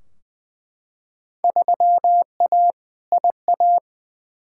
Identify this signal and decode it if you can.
Morse code, heard as a keyed tone, about 20 wpm, 700 Hz. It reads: 3A IA
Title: Field Day Exchanges Encore - 20wpm